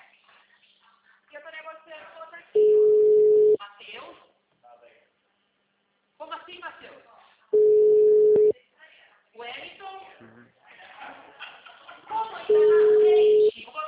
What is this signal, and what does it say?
Ringback tone (Europe, 425 Hz), with speech.
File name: out- -2018- - - .66295